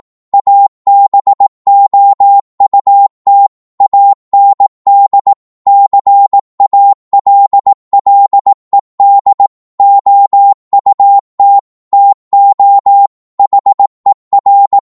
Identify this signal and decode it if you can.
Morse code, heard as a keyed tone, about 18 wpm, 800 Hz. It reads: ABOUT AND CALLED OUT TO HER